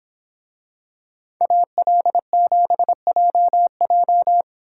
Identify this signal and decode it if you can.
Morse code, heard as a keyed tone, about 27 wpm, 700 Hz. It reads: AL7JJ
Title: Speed Racing - USA 2x2 Call Signs 22wpm